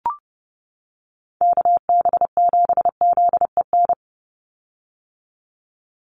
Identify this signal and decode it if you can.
Morse code, heard as a keyed tone, about 30 wpm, 700 Hz. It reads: KB7ZEN
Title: USA Call Signs 30wpm